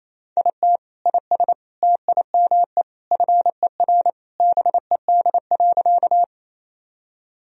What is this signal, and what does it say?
Morse code, heard as a keyed tone, about 28 wpm, 700 Hz. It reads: IT IS TIME FER BED.
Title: Sentences from Top 500 Words and Abbreviations 28wpm